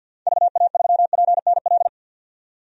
Morse code, heard as a keyed tone, about 51 wpm, 700 Hz. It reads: VA3PNL
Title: DXCC-100 Call Signs - 6 Characters - 50wpm